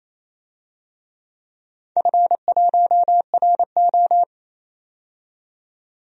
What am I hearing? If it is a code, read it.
Morse code, heard as a keyed tone, about 28 wpm, 700 Hz. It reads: F1RO
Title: DXCC-100 Call Signs - 4 Characters - 28wpm